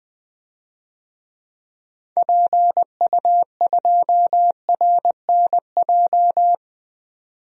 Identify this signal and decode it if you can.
Morse code, heard as a keyed tone, about 21 wpm, 700 Hz. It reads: PU2RNJ
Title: DXCC-100 Call Signs - 6 Characters - 20wpm